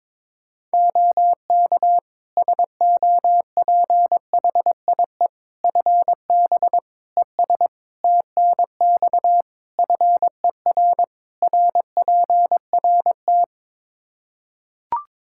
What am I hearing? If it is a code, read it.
Morse code, heard as a keyed tone, about 22 wpm, 700 Hz. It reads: OK SOPHIE FB ES TNX FER RPRT
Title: Practice QSO - Exchange 2 - Info - 22wpm